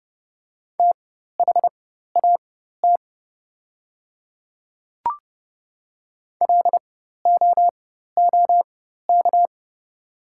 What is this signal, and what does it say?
Morse code, heard as a keyed tone, about 31 wpm, spaced as for 14 wpm, 690 Hz. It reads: THAT LOOK
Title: 4x Character Spacing - Top 100 Words - 30wpm